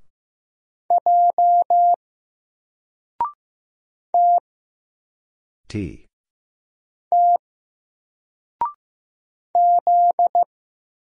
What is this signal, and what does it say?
Morse code, heard as a keyed tone, about 15 wpm, 700 Hz. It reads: J T T Z
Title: Single Character (Letter:Number:Punctuation:Prosign) 15wpm